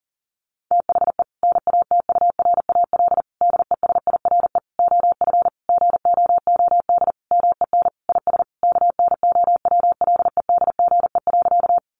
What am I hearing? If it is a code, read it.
Morse code, heard as a keyed tone, about 40 wpm, 700 Hz. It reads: THE NATURAL DESIRE OF GOOD MEN IS KNOWLEDGE.
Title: Sentences from Top 1100 Words 40wpm